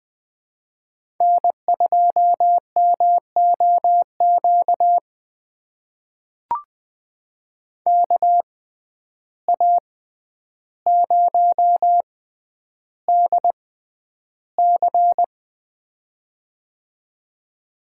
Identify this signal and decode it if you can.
Morse code, heard as a keyed tone, about 20 wpm, spaced as for 10 wpm, 700 Hz. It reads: N2MOQ KA0DC
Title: USA Call Signs - 5 Characters - 6x Character Spacing - 20wpm